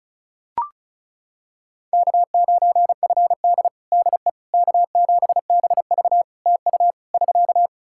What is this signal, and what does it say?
Morse code, heard as a keyed tone, about 34 wpm, 700 Hz. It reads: K9FD DE K7BV TU <SK>
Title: Practice QSO - Exchange 4 - Ending - 35wpm